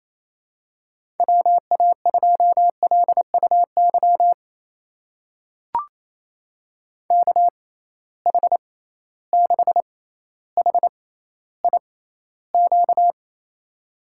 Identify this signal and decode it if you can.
Morse code, heard as a keyed tone, about 27 wpm, spaced as for 14 wpm, 700 Hz. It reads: WA2LUY KH6HIQ
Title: USA Call Signs - 6 Characters - 6x Character Spacing - 28wpm